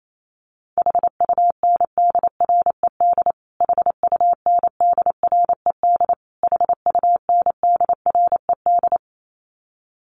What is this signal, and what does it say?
Morse code, heard as a keyed tone, about 28 wpm, 700 Hz. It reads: HUNDRED HUNDRED HUNDRED
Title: Top 500 Words Repeated 28wpm